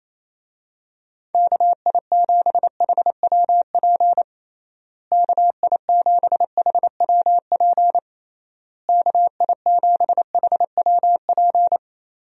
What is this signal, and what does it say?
Morse code, heard as a keyed tone, about 28 wpm, 700 Hz. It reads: KI7HWP KI7HWP KI7HWP
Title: USA Call Signs - 6 Characters - Repeated 3x - 28wpm